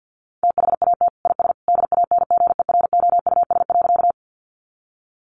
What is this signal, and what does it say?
Morse code, heard as a keyed tone, about 50 wpm, 700 Hz. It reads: THAT IS DANGEROUS.
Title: Sentences from Top 1200 Words 50wpm